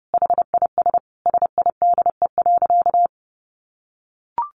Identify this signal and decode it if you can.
Morse code, heard as a keyed tone, about 30 wpm, 700 Hz. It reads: HIS SIDE.
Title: A Sound Of Thunder 30wpm